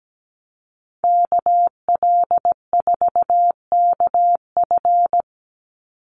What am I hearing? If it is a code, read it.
Morse code, heard as a keyed tone, about 17 wpm, 700 Hz. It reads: KL4KF
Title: USA Call Signs 17wpm